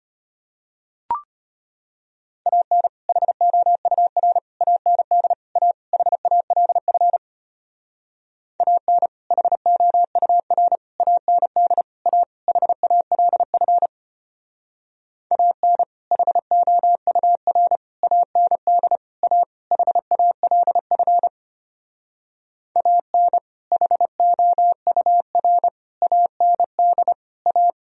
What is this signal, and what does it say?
Morse code, heard as a keyed tone, about 33 wpm, 700 Hz. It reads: AN HOUR AND A HALF AN HOUR AND A HALF AN HOUR AND A HALF AN HOUR AND A